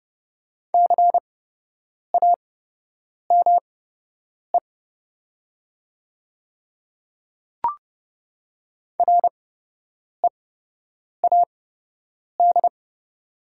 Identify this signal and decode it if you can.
Morse code, heard as a keyed tone, about 30 wpm, spaced as for 8 wpm, 700 Hz. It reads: CAME READ
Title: 8x Character Spacing - Top 300 Words - 30wpm